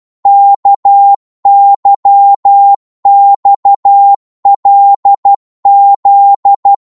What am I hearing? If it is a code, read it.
Morse code, heard as a keyed tone, about 12 wpm, 800 Hz. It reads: KYXLZ
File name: Morse / ltr-12wpm